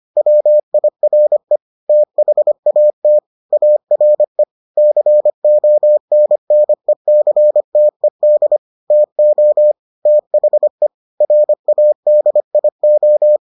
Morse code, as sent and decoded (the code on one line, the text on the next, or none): .-- .. .-. / - .... .- - / .- .-. / -.-. --- -. -. -.-. - . -.. / - --- / - .... / .-. .- -.. .. ---
WIRE THAT ARE CONNECTED TO THE RADIO